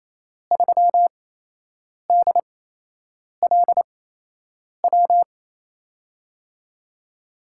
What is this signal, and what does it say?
Morse code, heard as a keyed tone, about 28 wpm, 700 Hz. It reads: 3 D L W
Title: USA Call Signs - 6 Characters - 8x Character Spacing - 28wpm